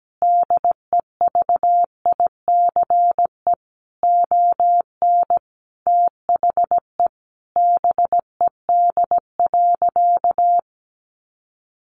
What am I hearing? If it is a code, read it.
Morse code, heard as a keyed tone, about 17 wpm, 700 Hz. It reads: DEVICE ON THE BED.
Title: Sentences from Top 1500 Words 17wpm